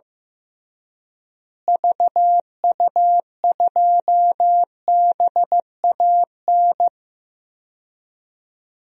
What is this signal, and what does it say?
Morse code, heard as a keyed tone, about 15 wpm, 700 Hz. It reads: VU2BAN
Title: DXCC-100 Call Signs - 6 Characters - Repeated 3x - 15wpm